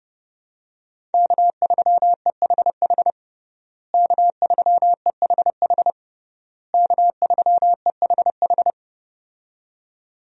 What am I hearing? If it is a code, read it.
Morse code, heard as a keyed tone, about 32 wpm, 700 Hz. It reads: K3EHH K3EHH K3EHH
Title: USA Call Signs - 5 Characters - Repeated 3x - 30wpm